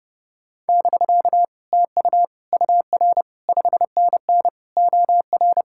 Morse code, heard as a keyed tone, about 30 wpm, 700 Hz. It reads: <BK> TU UR 5NN OR